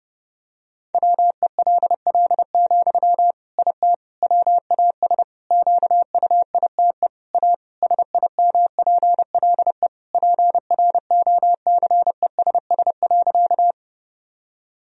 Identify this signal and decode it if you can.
Morse code, heard as a keyed tone, about 32 wpm, 700 Hz. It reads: WELL, IT WAS QUITE A SIMPLE PROCESS.
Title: Sentences from Top 1000 Words - Review Part 3 - 30wpm